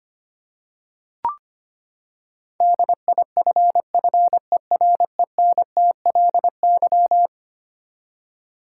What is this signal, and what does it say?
Morse code, heard as a keyed tone, about 25 wpm, 700 Hz. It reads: DIFFERENTLY
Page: morse-code-ninja-practice-25wpm